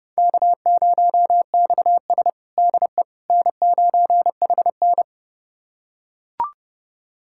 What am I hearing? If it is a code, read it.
Morse code, heard as a keyed tone, about 30 wpm, 700 Hz. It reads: K0XS DE N9HN
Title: USA Callsign DE Callsign - 4 Characters - 30wpm